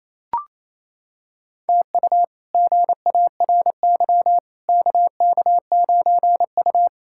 Morse code, heard as a keyed tone, about 28 wpm, 700 Hz. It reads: TU GARY KK9U
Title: K1USN SST QSO Practice - 28wpm